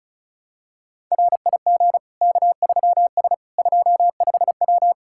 Morse code, heard as a keyed tone, about 34 wpm, 700 Hz. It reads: RIG K3S 25W
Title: K1USN SST QSO Practice 35wpm